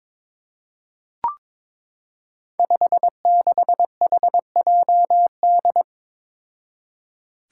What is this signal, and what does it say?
Morse code, heard as a keyed tone, about 22 wpm, 700 Hz. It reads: 56HJD
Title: Five Challenging Letter-Number Combinations 22wpm